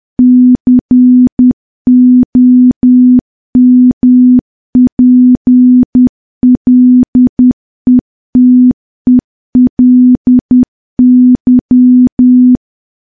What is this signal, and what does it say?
Morse code, heard as a keyed tone, about 10 wpm, 250 Hz. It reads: COMPLETELY
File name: morse / morse-training-10WPM